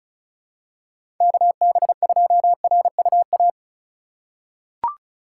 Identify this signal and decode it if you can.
Morse code, heard as a keyed tone, about 34 wpm, 700 Hz. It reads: KB2RUA
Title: USA Call Signs - 2x Character Spacing - 35wpm